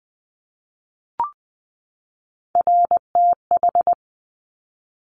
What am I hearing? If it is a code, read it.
Morse code, heard as a keyed tone, about 20 wpm, 700 Hz. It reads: RTH
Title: Top Three Letter Combinations in Words 20wpm